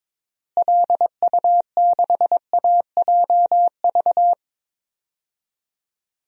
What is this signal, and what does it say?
Morse code, heard as a keyed tone, about 22 wpm, 700 Hz. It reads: LU6AJV
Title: DXCC-100 Call Signs - 6 Characters - Repeated 2x - 22wpm